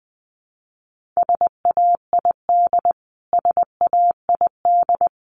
Morse code, heard as a keyed tone, about 20 wpm, 700 Hz. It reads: SAID SAID
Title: Top 200 Words Repeated 20wpm